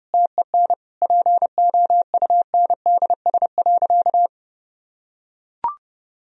Morse code, heard as a keyed tone, about 30 wpm, 700 Hz. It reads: TEN POUNDS.
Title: Sentences from Top 1000 Words - Review Part 4 - 30wpm